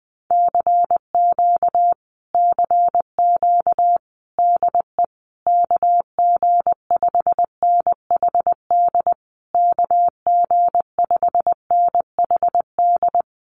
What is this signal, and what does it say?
Morse code, heard as a keyed tone, about 20 wpm, 700 Hz. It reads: CQ CQ DE KG5NHD KG5NHD
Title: Common QSO Phrases Encore 20wpm